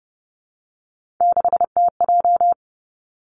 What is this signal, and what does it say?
Morse code, heard as a keyed tone, about 32 wpm, 700 Hz. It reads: 6TJ
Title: Three Challenging Letter-Number Combinations 30wpm